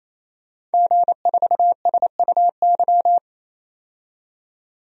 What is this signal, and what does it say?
Morse code, heard as a keyed tone, about 28 wpm, 700 Hz. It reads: G4SUY
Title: DXCC-100 Call Signs - 5 Characters - 28wpm